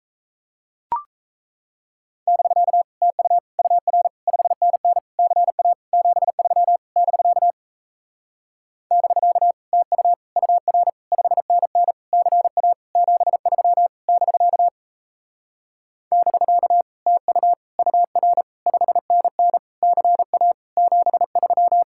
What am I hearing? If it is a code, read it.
Morse code, heard as a keyed tone, about 37 wpm, 700 Hz. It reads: <BK> TU UR 5NN CA 73 <BK> <BK> TU UR 5NN CA 73 <BK> <BK> TU UR 5NN CA 73